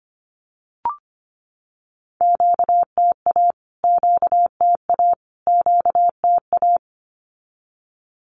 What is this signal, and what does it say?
Morse code, heard as a keyed tone, about 25 wpm, 700 Hz. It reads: QTA QTA QTA
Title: Q-Codes Repeated 3x 25wpm